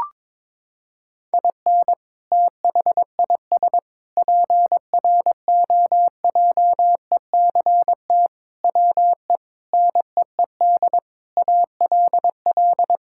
Morse code, heard as a keyed tone, about 22 wpm, 700 Hz. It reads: IN THIS PROJECT WE NEED ALL
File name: practice / 22wpm / Sentences with 8-Word Phrases - 22wpm